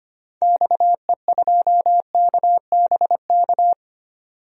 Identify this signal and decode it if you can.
Morse code, heard as a keyed tone, about 25 wpm, 700 Hz. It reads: XE2KBK
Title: DXCC-100 Call Signs - 6 Characters - 25wpm